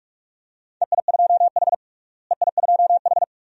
Morse code, heard as a keyed tone, about 46 wpm, 700 Hz. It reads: EI2H EI2H
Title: DXCC-100 Call Signs - 4 Characters - Repeated 2x - 45wpm